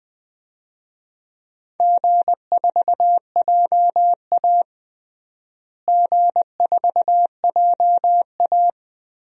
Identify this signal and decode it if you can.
Morse code, heard as a keyed tone, about 20 wpm, 700 Hz. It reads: G4JA G4JA
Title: DXCC-100 Call Signs - 4 Characters - Repeated 2x - 20wpm